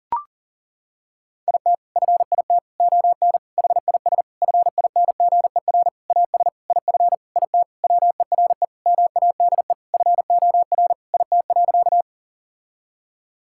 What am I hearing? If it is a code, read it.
Morse code, heard as a keyed tone, about 40 wpm, 700 Hz. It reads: IT FIT ON HIS FINGER AS IF IT WERE MADE FOR IT.
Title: Sentences from Top 1600 Words 40wpm